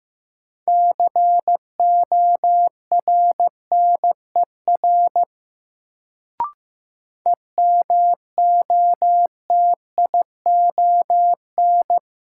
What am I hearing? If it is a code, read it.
Morse code, heard as a keyed tone, about 15 wpm, 700 Hz. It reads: CORNER EMOTION
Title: Sentences from Top 1600 Words - New Word Drill - 15wpm